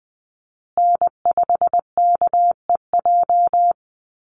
Morse code, heard as a keyed tone, about 20 wpm, 700 Hz. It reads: N5KEJ
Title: USA Call Signs Repeated 2x 20wpm